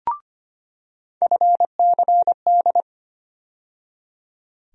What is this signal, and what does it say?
Morse code, heard as a keyed tone, about 25 wpm, 700 Hz. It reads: FCD
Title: USA Call Sign Postfixes - 3 Letters 25wpm